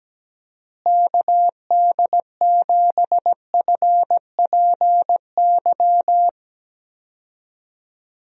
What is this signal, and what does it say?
Morse code, heard as a keyed tone, about 17 wpm, 700 Hz. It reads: KD7FPY
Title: USA Call Signs - 6 Characters - Repeated 3x - 17wpm